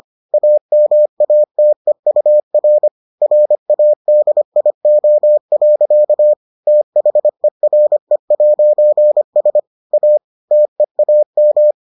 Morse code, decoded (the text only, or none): AMATEUR RADIO. THERE'S A TEAM